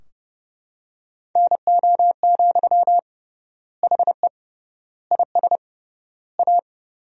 Morse code, heard as a keyed tone, about 30 wpm, 700 Hz. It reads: NO, HE IS A
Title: 3x Word-Spacing - Sentences from Top 400 Words - 30wpm